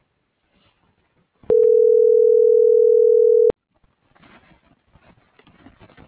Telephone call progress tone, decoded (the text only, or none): ringback tone